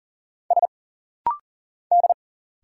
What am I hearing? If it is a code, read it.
Morse code, heard as a keyed tone, about 40 wpm, 700 Hz. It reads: S D
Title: Single Letters - Rapid-Fire 40wpm